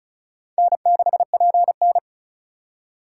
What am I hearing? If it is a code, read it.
Morse code, heard as a keyed tone, about 36 wpm, 700 Hz. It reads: N6PN
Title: USA Call Signs - 4 Characters - 3x Character Spacing - 35wpm